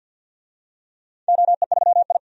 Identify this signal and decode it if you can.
Morse code, heard as a keyed tone, about 51 wpm, 700 Hz. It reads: OE3I
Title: DXCC-100 Call Signs - 4 Characters - 50wpm